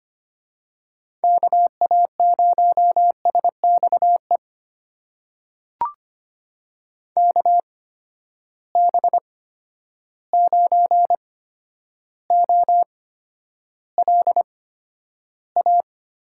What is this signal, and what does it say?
Morse code, heard as a keyed tone, about 25 wpm, spaced as for 11 wpm, 700 Hz. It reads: KA0SXE KB9OLA
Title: USA Call Signs - 6 Characters - 8x Character Spacing - 25wpm